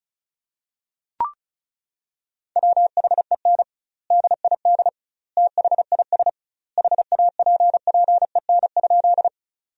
Morse code, heard as a keyed tone, about 34 wpm, 700 Hz. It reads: WHEN DID THIS HAPPEN?
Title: 2x Word-Spacing - Sentences from Top 400 Words - 35wpm